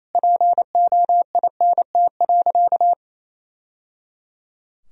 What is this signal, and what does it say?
Morse code, heard as a keyed tone, about 28 wpm, 700 Hz. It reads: POINT.
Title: Sentences from Top 1000 Words - Review Part 1 - 28wpm